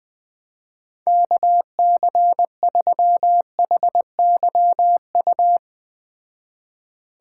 Morse code, heard as a keyed tone, about 20 wpm, 700 Hz. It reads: KC3HYU